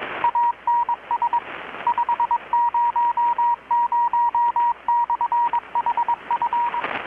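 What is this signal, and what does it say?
Morse code, heard as a keyed tone, about 21 wpm, 950 Hz, with radio noise. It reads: ANS 500/HF
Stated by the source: Morse from KPH's Night of Nights XX maritime radiotelegraphy event, received in Chiapas, Mexico about 3,800 km away on a Kenwood R-600 with a grounded 10 m long-wire antenna.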